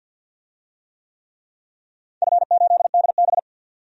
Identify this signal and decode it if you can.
Morse code, heard as a keyed tone, about 51 wpm, 700 Hz. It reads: F8DB
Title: DXCC-100 Call Signs - 4 Characters - 50wpm